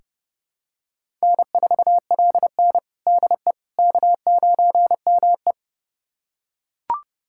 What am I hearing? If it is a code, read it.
Morse code, heard as a keyed tone, about 30 wpm, 700 Hz. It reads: N4LN DE K9ME
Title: USA Callsign DE Callsign - 4 Characters - 30wpm